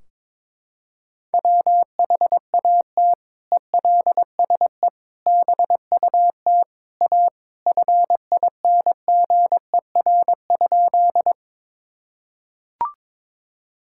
Morse code, heard as a keyed tone, about 22 wpm, 700 Hz. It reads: WHAT ELSE BUT A FINGER?
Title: Sentences from Top 1600 Words 22wpm